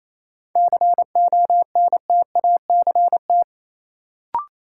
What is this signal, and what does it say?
Morse code, heard as a keyed tone, about 28 wpm, 700 Hz. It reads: CONTACT
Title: Sentences from Top 1000 Words - Review Part 7 - New Word Drill - 28wpm